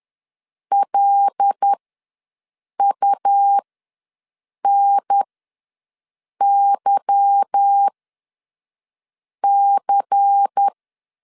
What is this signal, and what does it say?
Morse code, heard as a keyed tone, about 10 wpm, spaced as for 5 wpm, 800 Hz. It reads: LUNYC
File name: LEZIONI / 9 2867 ws 10wpm 4215